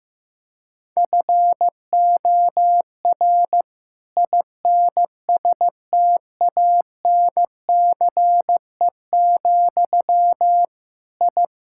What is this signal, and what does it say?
Morse code, heard as a keyed tone, about 15 wpm, 700 Hz. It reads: FOR INSTANCE, I
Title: Sentences from Top 1000 Words - Review Part 5 - 15wpm